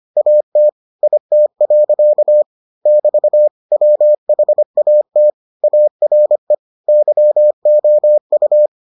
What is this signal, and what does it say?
Morse code, heard as a keyed tone, about 25 wpm, 600 Hz. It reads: AT IT. <BT> WHAT ARE YOU